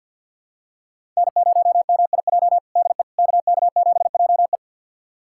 Morse code, heard as a keyed tone, about 51 wpm, 700 Hz. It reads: N0MIJ DE KK7JE
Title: USA Callsign DE Callsign - 5 Characters - 50wpm